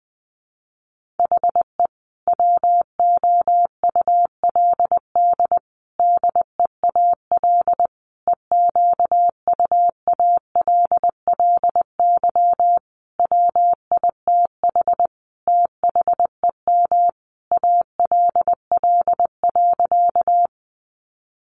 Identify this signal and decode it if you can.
Morse code, heard as a keyed tone, about 20 wpm, 700 Hz. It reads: HE WOULD DEAL EQUALLY WITH THEM ALL.
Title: Sentences from Top 1400 Words 20wpm